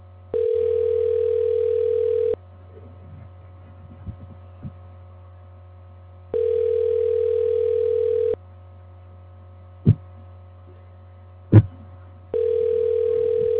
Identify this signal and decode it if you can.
Ringback tone (North America), 440 + 480 Hz.